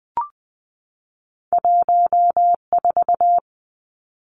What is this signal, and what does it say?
Morse code, heard as a keyed tone, about 20 wpm, 700 Hz. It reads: 14
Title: Two Challenging Letter-Number Combinations 20wpm